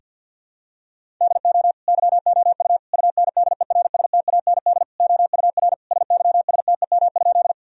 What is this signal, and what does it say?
Morse code, heard as a keyed tone, about 52 wpm, 700 Hz. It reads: DO YOU UNDERSTAND OUR SYSTEM?
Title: Sentences from Top 1000 Words - Review Part 3 - 50wpm